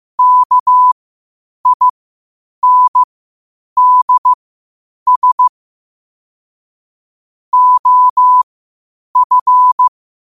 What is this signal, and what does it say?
Morse code, heard as a keyed tone, about 15 wpm, spaced as for 8 wpm, 1000 Hz. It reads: KINDS OF
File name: 8wpm